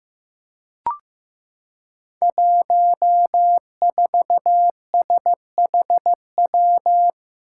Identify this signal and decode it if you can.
Morse code, heard as a keyed tone, about 15 wpm, 700 Hz. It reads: 14SHW
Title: Five Challenging Letter-Number Combinations 15wpm